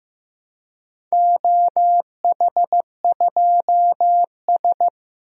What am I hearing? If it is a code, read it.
Morse code, heard as a keyed tone, about 15 wpm, 700 Hz. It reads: OH2S
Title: DXCC-100 Call Signs - 4 Characters - Repeated 2x - 15wpm